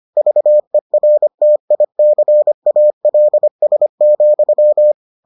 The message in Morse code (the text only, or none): VERTICALS,